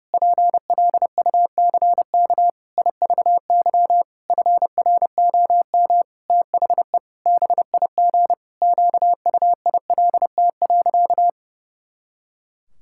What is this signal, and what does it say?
Morse code, heard as a keyed tone, about 30 wpm, 700 Hz. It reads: PLUCK IVY FROM THE BIG QUILT.
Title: Pangram Challenge v2 - 30wpm